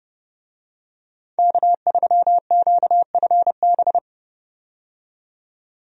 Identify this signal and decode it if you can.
Morse code, heard as a keyed tone, about 32 wpm, 700 Hz. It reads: K3QFB
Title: USA Call Signs - 5 Characters - 30wpm